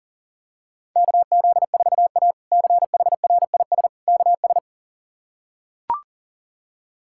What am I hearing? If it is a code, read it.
Morse code, heard as a keyed tone, about 40 wpm, 700 Hz. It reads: KZ4A CHRIS KS
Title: K1USN SST QSO Practice - 40wpm